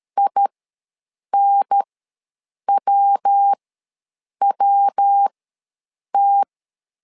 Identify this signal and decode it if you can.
Morse code, heard as a keyed tone, about 13 wpm, spaced as for 7 wpm, 800 Hz. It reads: INWWT
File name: LEZIONI / 3 MWRO / 3 MW ws 12wpm 1816